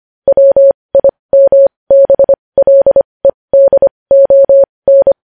Morse code, heard as a keyed tone, about 25 wpm, spaced as for 20 wpm, 550 Hz. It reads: WIMBLEDON
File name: podcast / News headlines-1x-25x20